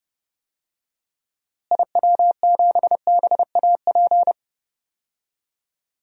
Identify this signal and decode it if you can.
Morse code, heard as a keyed tone, about 31 wpm, 700 Hz. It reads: IW7BAP
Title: DXCC-100 Call Signs - 6 Characters - Repeated 3x - 30wpm